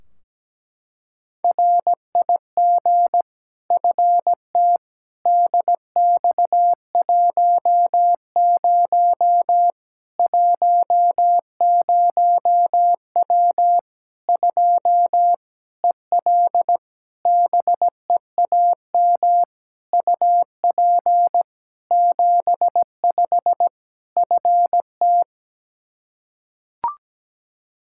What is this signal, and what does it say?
Morse code, heard as a keyed tone, about 17 wpm, 700 Hz. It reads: RIG FT DX10 10W 2 EL BEAM UP 75 FT
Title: K1USN SST QSO Practice 17wpm